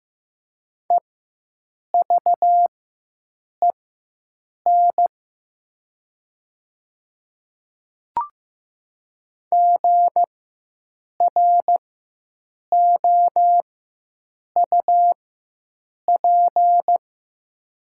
Morse code, heard as a keyed tone, about 15 wpm, spaced as for 7 wpm, 700 Hz. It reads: EVEN GROUP
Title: 4x Character Spacing - Top 500 Words - 15wpm